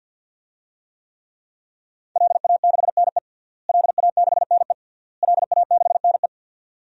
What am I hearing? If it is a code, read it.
Morse code, heard as a keyed tone, about 48 wpm, 700 Hz. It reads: LA6NE LA6NE LA6NE
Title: DXCC-100 Call Signs - 5 Characters - Repeated 3x - 50wpm